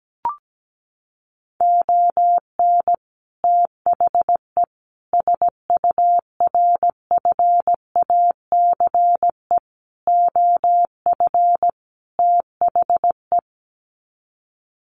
Morse code, heard as a keyed tone, about 17 wpm, 700 Hz. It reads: ON THE SURFACE OF THE
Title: Sets of 5 Words Encore 17wpm